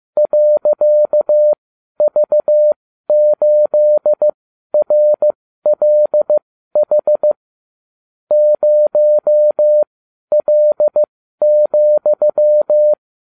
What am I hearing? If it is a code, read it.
Morse code, heard as a keyed tone, about 15 wpm, spaced as for 12 wpm, 600 Hz. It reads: .V8RLH 0L,